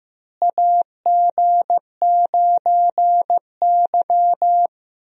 Morse code, heard as a keyed tone, about 15 wpm, 700 Hz. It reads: AG9Y
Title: USA Call Signs - 4 Characters - Repeated 2x - 15wpm